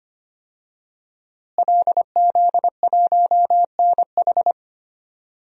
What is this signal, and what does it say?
Morse code, heard as a keyed tone, about 25 wpm, 700 Hz. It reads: LZ1NH
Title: DXCC-100 Call Signs - 5 Characters - Repeated 2x - 25wpm